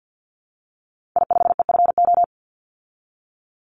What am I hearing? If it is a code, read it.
Morse code, heard as a keyed tone, about 50 wpm, 700 Hz. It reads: I5EFO
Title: Call Signs Repeated 3x - 50wpm